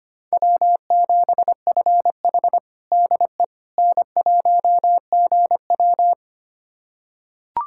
Morse code, heard as a keyed tone, about 25 wpm, 700 Hz. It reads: W7FH DE N1GW
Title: USA Callsign DE Callsign - 4 Characters - 25wpm